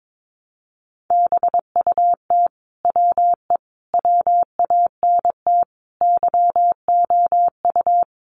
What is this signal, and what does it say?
Morse code, heard as a keyed tone, about 22 wpm, 700 Hz. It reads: BUT WE WANT YOU
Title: Sentences from Top 700 Words 22wpm